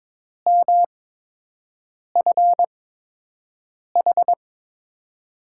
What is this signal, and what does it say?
Morse code, heard as a keyed tone, about 22 wpm, spaced as for 6 wpm, 700 Hz. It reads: MFH
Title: USA Call Signs - 6 Characters - 8x Character Spacing - 22wpm